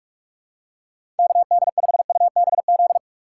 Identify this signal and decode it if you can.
Morse code, heard as a keyed tone, about 44 wpm, 700 Hz. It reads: KD5UBZ
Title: USA Call Signs - 6 Characters - 6x Character Spacing - 45wpm